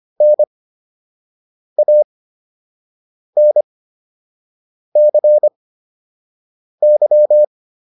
Morse code, heard as a keyed tone, about 25 wpm, spaced as for 6 wpm, 600 Hz. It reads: NANCY